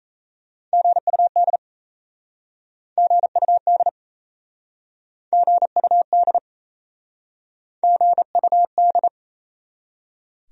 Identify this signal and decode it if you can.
Morse code, heard as a keyed tone, about 39 wpm, 700 Hz. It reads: GUD GUD GUD GUD